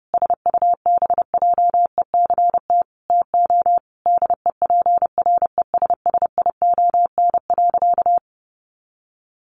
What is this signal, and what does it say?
Morse code, heard as a keyed tone, about 30 wpm, 700 Hz. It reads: SUBJECT TO DEPRESSION.
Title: Sentences from Top 1300 Words 30wpm